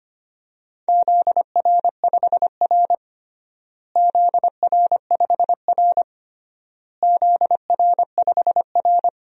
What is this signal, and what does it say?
Morse code, heard as a keyed tone, about 25 wpm, 700 Hz. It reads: ZR5R ZR5R ZR5R
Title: DXCC-100 Call Signs - 4 Characters - Repeated 3x - 25wpm